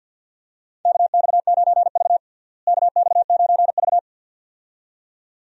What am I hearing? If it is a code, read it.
Morse code, heard as a keyed tone, about 50 wpm, 700 Hz. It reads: KX9V KX9V
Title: USA Call Signs - 4 Characters - Repeated 2x - 50wpm